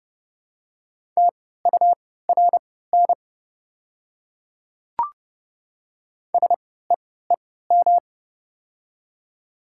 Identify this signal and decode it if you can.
Morse code, heard as a keyed tone, about 31 wpm, spaced as for 17 wpm, 700 Hz. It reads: TURN SEEM
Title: 3x Character Spacing - Top 300 Words - 30wpm